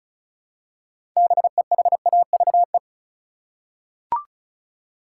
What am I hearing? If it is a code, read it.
Morse code, heard as a keyed tone, about 36 wpm, 700 Hz. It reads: BEHAVE
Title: Six Letter Words - 35wpm